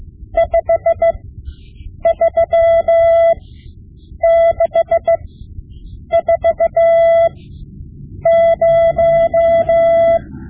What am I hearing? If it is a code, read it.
Morse code, heard as a keyed tone, about 16 wpm, spaced as for 7 wpm, 670 Hz. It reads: 53640
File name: Morse Lessons / Morse Lessons / Tape-5B